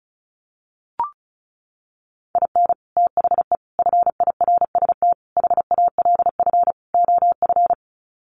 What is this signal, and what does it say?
Morse code, heard as a keyed tone, about 34 wpm, 700 Hz. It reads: IN THE FIRST HALF OF
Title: Sets of 5 Words 35wpm